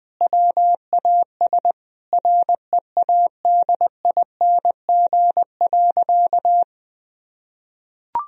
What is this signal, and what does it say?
Morse code, heard as a keyed tone, about 20 wpm, 700 Hz. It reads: WAS READING.
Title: Sentences from Top 1000 Words - Review Part 3 - 20wpm